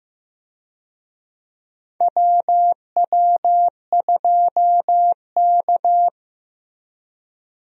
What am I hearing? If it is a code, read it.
Morse code, heard as a keyed tone, about 15 wpm, 700 Hz. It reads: WW2K
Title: USA Call Signs - 4 Characters - Repeated 3x - 15wpm